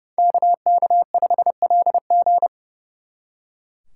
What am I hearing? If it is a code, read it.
Morse code, heard as a keyed tone, about 37 wpm, 700 Hz. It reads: KK5LG
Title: Speed Racing - USA 2x2 Call Signs 30wpm